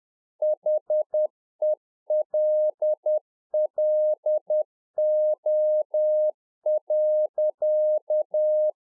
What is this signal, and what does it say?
Morse code, heard as a keyed tone, about 10 wpm, 610 Hz. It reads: HELLO.